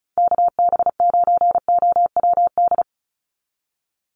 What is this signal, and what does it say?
Morse code, heard as a keyed tone, about 34 wpm, 700 Hz. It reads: KB9OWD
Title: Call Signs - 35wpm